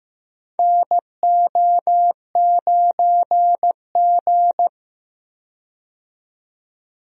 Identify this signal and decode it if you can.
Morse code, heard as a keyed tone, about 15 wpm, 700 Hz. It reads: NO9G
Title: USA Call Signs - 4 Characters - 15wpm